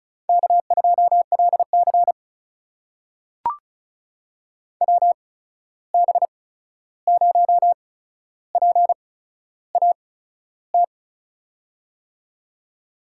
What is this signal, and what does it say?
Morse code, heard as a keyed tone, about 36 wpm, spaced as for 13 wpm, 700 Hz. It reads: K2LC WB0PAT
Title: USA Call Signs - 8x Character Spacing - 35wpm